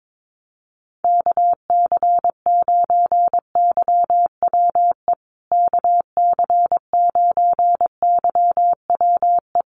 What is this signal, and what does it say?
Morse code, heard as a keyed tone, about 22 wpm, 700 Hz. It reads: KC9YWE KC9YWE
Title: USA Call Signs Repeated 3x 22wpm